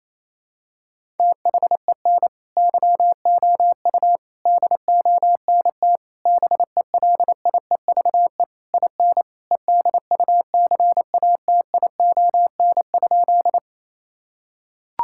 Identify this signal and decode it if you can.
Morse code, heard as a keyed tone, about 28 wpm, 700 Hz. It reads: THEN YOU DONT BELIEVE IN EDUCATION?
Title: Sentences from Top 1000 Words - Review Part 6 - 28wpm